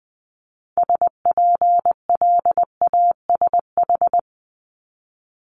Morse code, heard as a keyed tone, about 20 wpm, 700 Hz. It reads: SPLASH
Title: Multi-letter Phonemes 20wpm